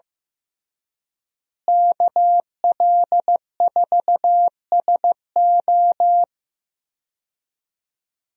Morse code, heard as a keyed tone, about 15 wpm, 700 Hz. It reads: KL4SO
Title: USA Call Signs - 5 Characters - Repeated 3x - 15wpm